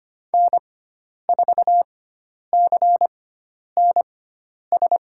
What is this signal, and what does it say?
Morse code, heard as a keyed tone, about 25 wpm, spaced as for 10 wpm, 700 Hz. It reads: N4CNS
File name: Lesson 16g - Letter c - Call Signs 25wpm